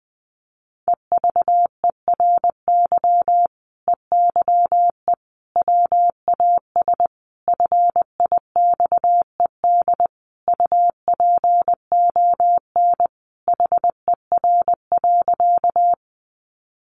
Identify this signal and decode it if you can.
Morse code, heard as a keyed tone, about 20 wpm, 700 Hz. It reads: EVERY EYE WAS FIXED UPON HER.
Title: Sentences from Top 1100 Words 20wpm